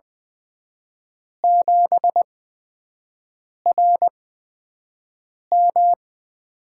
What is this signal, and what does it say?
Morse code, heard as a keyed tone, about 20 wpm, spaced as for 5 wpm, 700 Hz. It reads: 7RM
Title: USA Call Signs - 8x Character Spacing - 20wpm